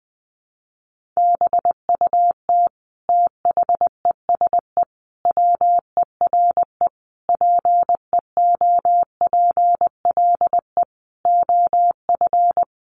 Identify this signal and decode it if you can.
Morse code, heard as a keyed tone, about 20 wpm, 700 Hz. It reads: BUT THESE WERE PEOPLE OF
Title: Sentences from Top 1500 Words 20wpm